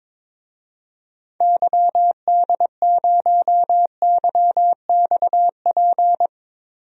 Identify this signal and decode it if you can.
Morse code, heard as a keyed tone, about 22 wpm, 700 Hz. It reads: YD0YXP
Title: DXCC-100 Call Signs - 6 Characters - Repeated 2x - 22wpm